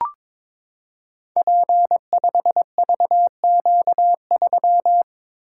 Morse code, heard as a keyed tone, about 22 wpm, 690 Hz. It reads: P5VQ3
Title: Code Groups - 5 Letter:Numbers - 22wpm